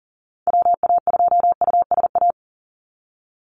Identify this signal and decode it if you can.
Morse code, heard as a keyed tone, about 40 wpm, 700 Hz. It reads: WA2USA
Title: NA SOTA Chaser Callsigns 40wpm